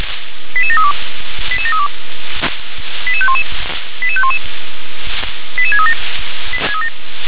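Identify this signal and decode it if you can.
Five-tone selective call (ZVEI), heard as quick runs of five tones, tones of about 70 ms. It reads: 88732 88732 88511 88511 88637
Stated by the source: ZVEI five-tone selective call, code 88513, received by SDR.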